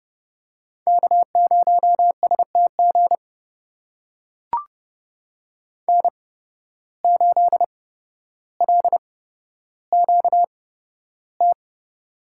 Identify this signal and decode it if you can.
Morse code, heard as a keyed tone, about 31 wpm, spaced as for 13 wpm, 700 Hz. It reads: K0STG N8LQT
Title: USA Call Signs - 8x Character Spacing - 30wpm